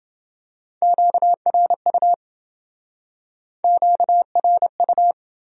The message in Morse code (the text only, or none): QRU QRU